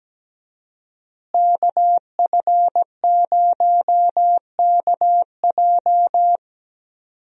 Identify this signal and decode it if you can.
Morse code, heard as a keyed tone, about 17 wpm, 700 Hz. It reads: KF0KJ
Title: USA Call Signs - 5 Characters - Repeated 2x - 17wpm